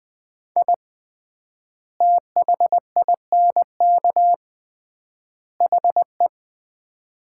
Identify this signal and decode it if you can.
Morse code, heard as a keyed tone, about 20 wpm, 700 Hz. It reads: I THINK HE
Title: 3x Word-Spacing - Sets of 3 Words - 20wpm